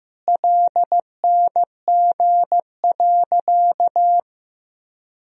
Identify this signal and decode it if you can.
Morse code, heard as a keyed tone, about 16 wpm, 700 Hz. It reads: LNG.
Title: Sentences from Top 100 Words and Abbreviations 15wpm